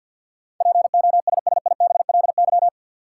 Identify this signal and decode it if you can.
Morse code, heard as a keyed tone, about 50 wpm, 700 Hz. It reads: POSSIBLY
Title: Sentences from Top 1300 Words - New Word Drill - 50wpm